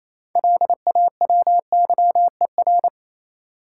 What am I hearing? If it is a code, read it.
Morse code, heard as a keyed tone, about 28 wpm, 700 Hz. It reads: LAWYER
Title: Six Letter Words - 28wpm